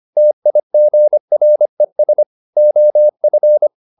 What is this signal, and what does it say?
Morse code, heard as a keyed tone, about 25 wpm, 600 Hz. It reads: TIGRES OF
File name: the old man and the sea part 1 25wpm